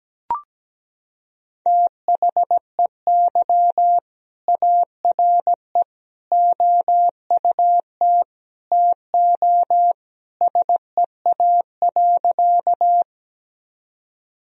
Morse code, heard as a keyed tone, about 17 wpm, 700 Hz. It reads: THEY ARE OUT TO SEA.
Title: Sentences from Top 1000 Words - Review Part 1 - 17wpm